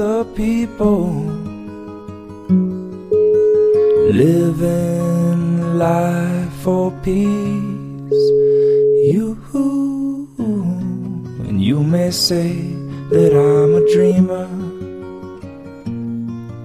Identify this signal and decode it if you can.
Ringback tone (Europe, 425 Hz).